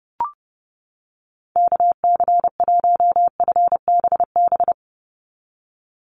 Morse code, heard as a keyed tone, about 32 wpm, 700 Hz. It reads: KC1FBB
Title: USA Call Signs 30wpm